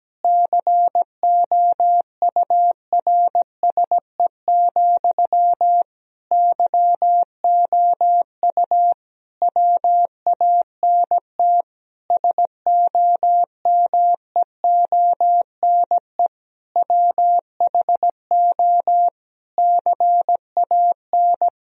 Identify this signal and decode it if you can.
Morse code, heard as a keyed tone, about 17 wpm, 700 Hz. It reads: COURSE, YOU WANT SOMEONE WHO CAN
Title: Sentences from Top 1000 Words - Review Part 1 - 17wpm